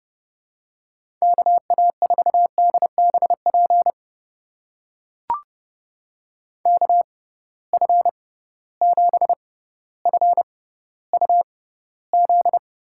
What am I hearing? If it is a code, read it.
Morse code, heard as a keyed tone, about 30 wpm, spaced as for 15 wpm, 700 Hz. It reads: KA4DBP KF7FUZ
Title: USA Call Signs - 6 Characters - 6x Character Spacing - 30wpm